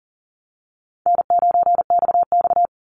Morse code, heard as a keyed tone, about 38 wpm, 700 Hz. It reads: N9XX
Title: Easy Call Signs Repeated 2x - 40wpm